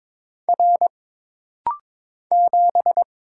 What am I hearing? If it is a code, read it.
Morse code, heard as a keyed tone, about 22 wpm, 700 Hz. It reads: R 7
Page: This rapid-fire timing is geared to push your ICR (Instant Character Recognition) skills.